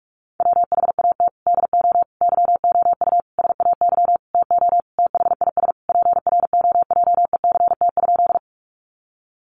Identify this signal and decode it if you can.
Morse code, heard as a keyed tone, about 44 wpm, 700 Hz. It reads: WHAT DO YOU SAY TO THIS PROJECT?
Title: Sentences from Top 1400 Words 45wpm